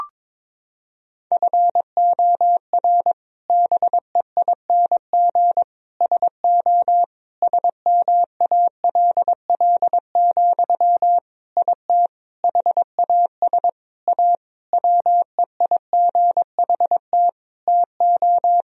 Morse code, heard as a keyed tone, about 22 wpm, 700 Hz. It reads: FOR BEING SO SMALL, IT HAS A WEIGHT TO
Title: Sentences from Top 1000 Words - Review Part 3 - 22wpm